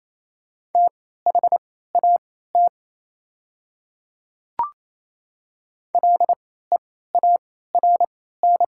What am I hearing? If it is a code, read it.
Morse code, heard as a keyed tone, about 27 wpm, spaced as for 16 wpm, 700 Hz. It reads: THAT LEARN
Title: 3x Character Spacing - Top 300 Words - 28wpm